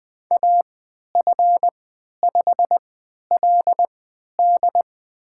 Morse code, heard as a keyed tone, about 20 wpm, spaced as for 11 wpm, 700 Hz. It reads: AF5LD
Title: USA Call Signs - 3x Character Spacing - 20wpm